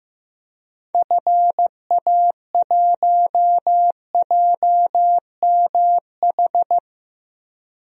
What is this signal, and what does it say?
Morse code, heard as a keyed tone, about 15 wpm, 700 Hz. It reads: FA1JMH
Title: DXCC-100 Call Signs - 6 Characters - 15wpm